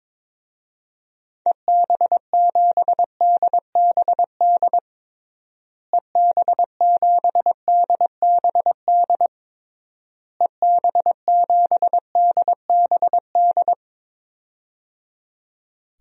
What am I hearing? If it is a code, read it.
Morse code, heard as a keyed tone, about 22 wpm, 700 Hz. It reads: EB7DBD EB7DBD EB7DBD
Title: DXCC-100 Call Signs - 6 Characters - Repeated 3x - 22wpm